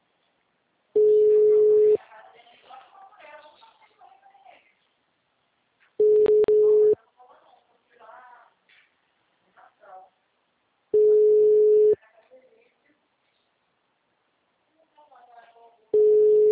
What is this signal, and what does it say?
Ringback tone (Europe, 425 Hz), with speech.